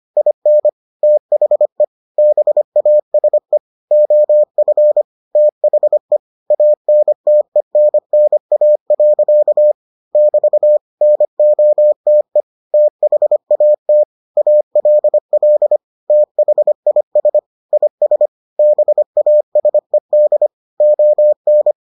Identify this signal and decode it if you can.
Morse code, heard as a keyed tone, about 25 wpm, 600 Hz. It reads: IN THE BASE OF THE ANTENNA. <BT> NOTE THAT ALL THIS IS BASED ON